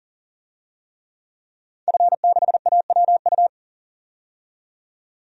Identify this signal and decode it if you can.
Morse code, heard as a keyed tone, about 39 wpm, 700 Hz. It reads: F6AWU
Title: DXCC-100 Call Signs - 5 Characters - 40wpm